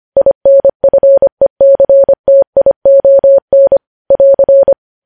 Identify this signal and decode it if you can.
Morse code, heard as a keyed tone, about 25 wpm, 550 Hz. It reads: INFECTION <AR>
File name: podcast / News headlines-1x-25x25